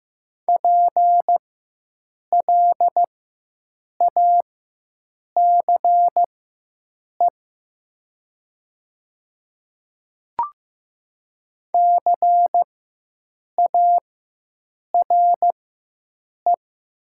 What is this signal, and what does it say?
Morse code, heard as a keyed tone, about 15 wpm, spaced as for 7 wpm, 700 Hz. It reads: PLACE CARE
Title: 4x Character Spacing - Top 500 Words - 15wpm